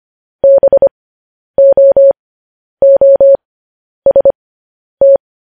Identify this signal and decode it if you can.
Morse code, heard as a keyed tone, about 25 wpm, spaced as for 10 wpm, 550 Hz. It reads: BOOST
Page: news-headlines-in-morse-code-at-10-wpm